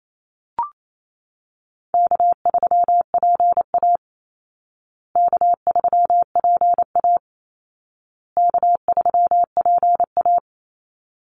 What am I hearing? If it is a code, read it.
Morse code, heard as a keyed tone, about 28 wpm, 700 Hz. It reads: K3PA K3PA K3PA
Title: Easy Call Signs Repeated 3x - 28wpm